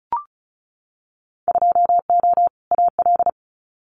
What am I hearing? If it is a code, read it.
Morse code, heard as a keyed tone, about 36 wpm, 700 Hz. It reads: 2O AL
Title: Winter Field Day Exchanges Encore 35wpm